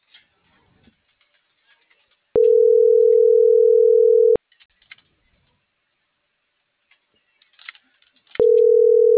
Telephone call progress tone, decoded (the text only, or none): ringback tone